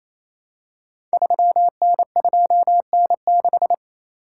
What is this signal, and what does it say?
Morse code, heard as a keyed tone, about 28 wpm, 700 Hz. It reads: 3N2N6
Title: Code Groups - 5 Letter:Numbers - 28wpm